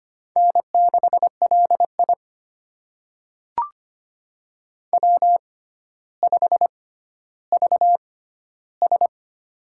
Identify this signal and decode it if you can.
Morse code, heard as a keyed tone, about 25 wpm, spaced as for 13 wpm, 700 Hz. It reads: N6LI W5VS
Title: USA Call Signs - 4 Characters - 6x Character Spacing - 25wpm